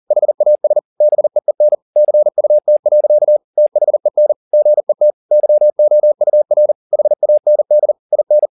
Morse code, heard as a keyed tone, about 40 wpm, 600 Hz. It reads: HAS BEEN CUT. THEN GET YOUR HAND IN
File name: the old man and the sea 2 40wpm